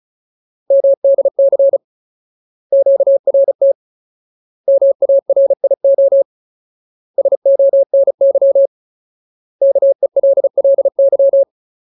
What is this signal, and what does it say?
Morse code, heard as a keyed tone, about 35 wpm, 550 Hz. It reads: MDC QRT MARIO SONY KELLY
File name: Ultimate Word List / 35WPM Ultimate